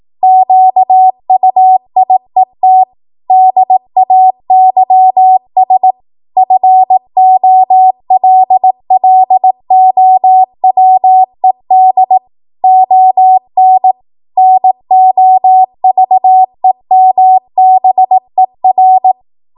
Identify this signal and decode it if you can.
Morse code, heard as a keyed tone, about 18 wpm, 750 Hz. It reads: QUIET DAYS FOLLOWED ON NOVEMBER
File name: Archive / 18 WPM / 18WPM